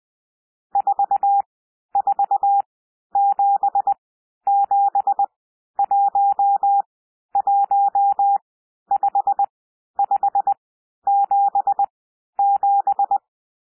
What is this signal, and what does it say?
Morse code, heard as a keyed tone, about 20 wpm, spaced as for 11 wpm, 800 Hz. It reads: 4477115577
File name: SOUND NUMBERS 20 wpm